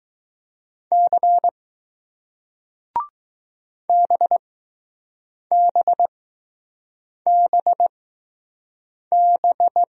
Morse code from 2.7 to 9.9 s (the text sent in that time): BBBB